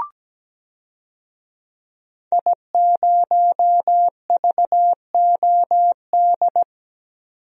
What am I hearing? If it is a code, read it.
Morse code, heard as a keyed tone, about 17 wpm, 700 Hz. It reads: I0VOD
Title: DXCC-100 Call Signs - 5 Characters - 17wpm